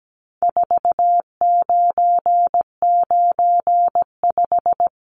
Morse code, heard as a keyed tone, about 17 wpm, 700 Hz. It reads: 4995
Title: Numbers - 4-Digit 17wpm